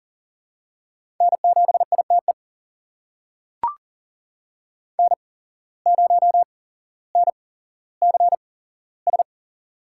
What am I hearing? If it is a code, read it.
Morse code, heard as a keyed tone, about 38 wpm, spaced as for 17 wpm, 700 Hz. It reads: N7ITE N0NCS
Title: USA Call Signs - 5 Characters - 8x Character Spacing - 40wpm